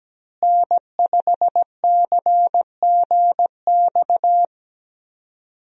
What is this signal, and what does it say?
Morse code, heard as a keyed tone, about 17 wpm, 700 Hz. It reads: N5CGX
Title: USA Call Signs - 5 Characters - Repeated 3x - 17wpm